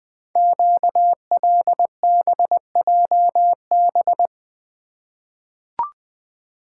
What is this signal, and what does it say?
Morse code, heard as a keyed tone, about 20 wpm, 700 Hz. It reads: QLBJB
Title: Code Groups - 5 Letters - 20wpm